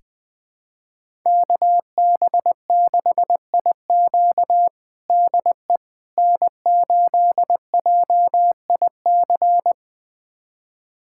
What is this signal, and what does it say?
Morse code, heard as a keyed tone, about 21 wpm, 700 Hz. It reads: KB6IQ DE N8JIC
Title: USA Callsign DE Callsign - 5 Characters - 20wpm